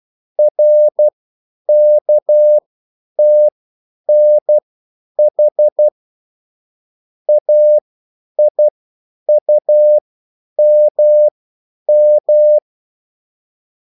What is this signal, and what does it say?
Morse code, heard as a keyed tone, about 12 wpm, spaced as for 9 wpm, 600 Hz. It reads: RKTNH AIUMM